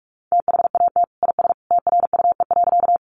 Morse code, heard as a keyed tone, about 44 wpm, 700 Hz. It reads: THAT IS TRUE.
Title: Sentences from Top 400 Words 45wpm